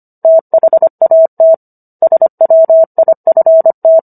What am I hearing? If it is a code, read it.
Morse code, heard as a keyed tone, about 25 wpm, 650 Hz. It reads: THAT SWIFT
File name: Book One / 16-Chapter 16 - 25 WPM